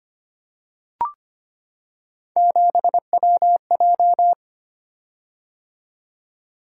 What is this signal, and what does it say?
Morse code, heard as a keyed tone, about 25 wpm, 700 Hz. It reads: 7WJ
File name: Three Letter-Number Combinations - 25wpm